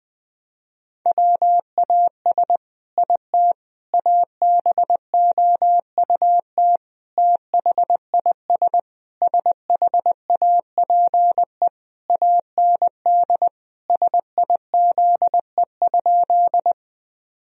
Morse code, heard as a keyed tone, about 20 wpm, 700 Hz. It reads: WAS IT ABOUT THIS SHAPE AND SIZE?
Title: Sentences from Top 1000 Words - Review Part 3 - 20wpm